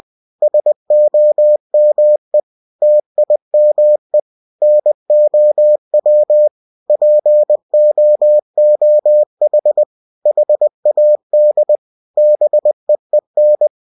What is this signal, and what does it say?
Morse code, heard as a keyed tone, about 20 wpm, 600 Hz. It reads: SOME TIME NOW POOH HAD BEEN
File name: cw-books / winnie-the-pooh / winnie-20wpm-page-0031